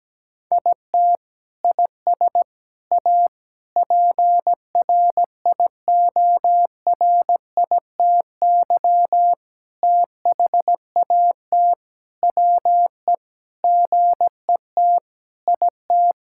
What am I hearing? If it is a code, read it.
Morse code, heard as a keyed tone, about 17 wpm, 700 Hz. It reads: IT IS A PRIORITY THAT WE GET IT
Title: Sentences from Top 1600 Words 17wpm